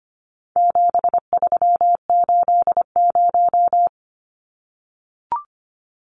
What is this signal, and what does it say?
Morse code, heard as a keyed tone, about 25 wpm, 700 Hz. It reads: 7380
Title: Numbers - 4-Digit 25wpm